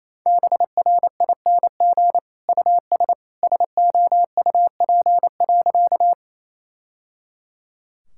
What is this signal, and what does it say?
Morse code, heard as a keyed tone, about 27 wpm, 700 Hz. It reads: BRING US SOUP.
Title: Sentences from Top 1000 Words - Review Part 6 - 28wpm